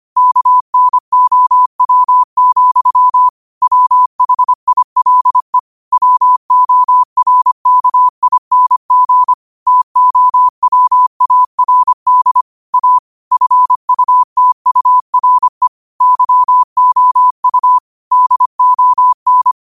Morse code, heard as a keyed tone, about 25 wpm, 1000 Hz. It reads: KNOW, WHILE WORKING TOWARD A FUTURE YOU DON
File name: qotdfiles / 25wpm